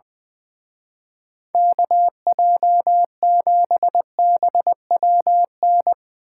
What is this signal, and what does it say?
Morse code, heard as a keyed tone, about 22 wpm, 700 Hz. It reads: KJ7BWN
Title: Speed Racing - USA 2x3 Call Signs 20wpm